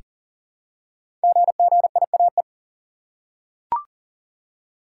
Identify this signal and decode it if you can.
Morse code, heard as a keyed tone, about 39 wpm, 690 Hz. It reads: GGIAE
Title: Code Groups - 5 Letters - 40wpm